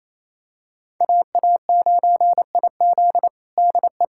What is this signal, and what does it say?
Morse code, heard as a keyed tone, about 28 wpm, 700 Hz. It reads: AA9IZ DE
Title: USA Callsign DE Callsign - 5 Characters - 28wpm